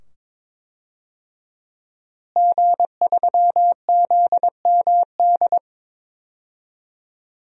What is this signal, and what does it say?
Morse code, heard as a keyed tone, about 22 wpm, 700 Hz. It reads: G3ZMD
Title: DXCC-100 Call Signs - 5 Characters - Repeated 2x - 22wpm